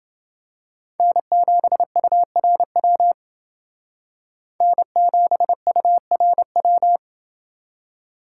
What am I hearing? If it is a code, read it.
Morse code, heard as a keyed tone, about 29 wpm, 700 Hz. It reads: N7URW N7URW